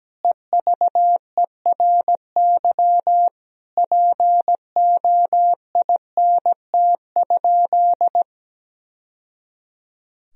Morse code, heard as a keyed tone, about 17 wpm, 700 Hz. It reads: EVERY POINT?
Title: Sentences from Top 1000 Words - Review Part 1 - 17wpm